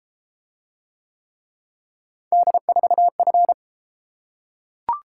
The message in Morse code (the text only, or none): D4F